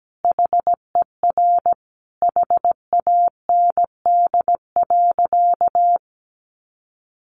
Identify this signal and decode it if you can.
Morse code, heard as a keyed tone, about 17 wpm, 700 Hz. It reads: HER HAND.
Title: Sentences from Top 500 Words 17wpm